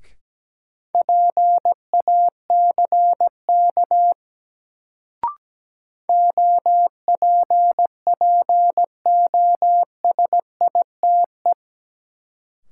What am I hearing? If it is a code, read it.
Morse code, heard as a keyed tone, about 17 wpm, 700 Hz. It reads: PACK OPPOSITE
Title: Sentences from Top 1300 Words - New Word Drill Alt - 17wpm